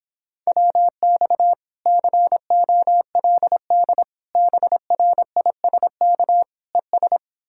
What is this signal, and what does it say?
Morse code, heard as a keyed tone, about 31 wpm, 700 Hz. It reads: WX COLD BRISK ES